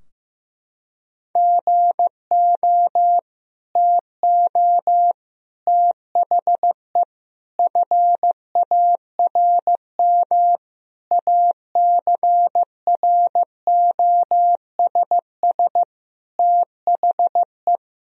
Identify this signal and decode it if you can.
Morse code, heard as a keyed tone, about 16 wpm, 700 Hz. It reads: GO TO THE FARM ACROSS THE
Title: Sentences from Top 1000 Words - Review Part 2 - 15wpm